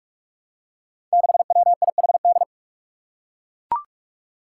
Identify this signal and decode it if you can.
Morse code, heard as a keyed tone, about 44 wpm, 700 Hz. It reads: 6WIHD
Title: Five Challenging Letter-Number Combinations 45wpm